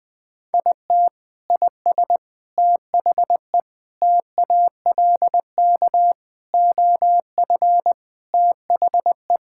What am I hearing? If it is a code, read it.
Morse code, heard as a keyed tone, about 20 wpm, 700 Hz. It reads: IT IS THE TALK OF THE
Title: Sentences with 4-Word Phrases - Spoken Beginning - 20wpm